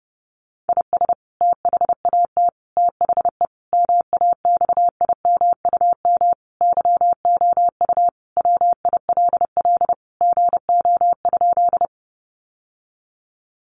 Morse code, heard as a keyed tone, about 30 wpm, 700 Hz. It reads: IS THAT THE MAXIMUM YOU WILL GO?
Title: Sentences from Top 1300 Words 30wpm